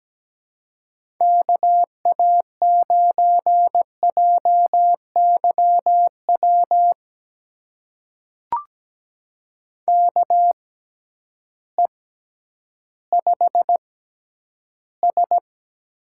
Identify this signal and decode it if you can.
Morse code, heard as a keyed tone, about 17 wpm, spaced as for 10 wpm, 700 Hz. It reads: KA9JYW KE5S
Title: USA Call Signs - 6 Characters - 6x Character Spacing - 17wpm